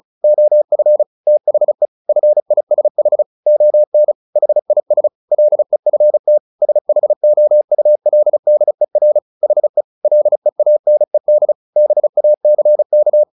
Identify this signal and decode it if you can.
Morse code, heard as a keyed tone, about 34 wpm, 600 Hz. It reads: OF THE FISH ON HIS LEFT SHOULDER HE LEANED BACK
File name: the old man and the sea / the old man and the sea 1 35wpm